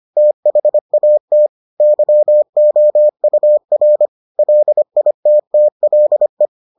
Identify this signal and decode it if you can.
Morse code, heard as a keyed tone, about 25 wpm, 600 Hz. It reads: THAT YOUR LITTLE